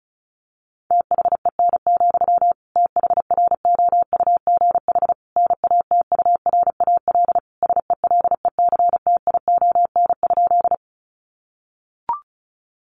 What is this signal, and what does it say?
Morse code, heard as a keyed tone, about 34 wpm, 700 Hz. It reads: THEN, THROUGH NATURAL SELECTION?
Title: Sentences from Top 1400 Words 35wpm